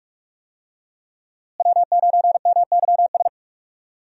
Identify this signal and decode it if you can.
Morse code, heard as a keyed tone, about 46 wpm, 700 Hz. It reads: W9MYS
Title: USA Call Signs - 5 Characters - Repeated 2x - 45wpm